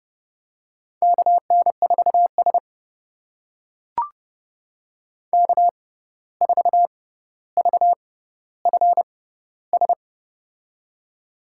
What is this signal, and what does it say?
Morse code, heard as a keyed tone, about 32 wpm, spaced as for 15 wpm, 700 Hz. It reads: KN4S K4VFS
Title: USA Call Signs - 6x Character Spacing - 30wpm